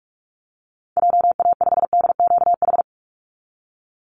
Morse code, heard as a keyed tone, about 44 wpm, 700 Hz. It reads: JA5DQH
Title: Call Signs Repeated 3x - 45wpm